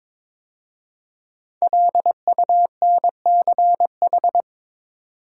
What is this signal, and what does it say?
Morse code, heard as a keyed tone, about 23 wpm, 700 Hz. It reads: LUNCH
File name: 5 Letter Words - First 3 Letters Given - 22wpm